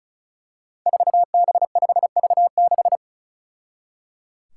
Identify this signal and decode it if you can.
Morse code, heard as a keyed tone, about 36 wpm, 700 Hz. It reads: 4B5V6
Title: Five Challenging Letter-Number Combinations 35wpm